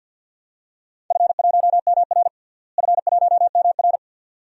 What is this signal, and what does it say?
Morse code, heard as a keyed tone, about 46 wpm, 700 Hz. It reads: F1MR F1MR
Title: DXCC-100 Call Signs - 4 Characters - Repeated 2x - 50wpm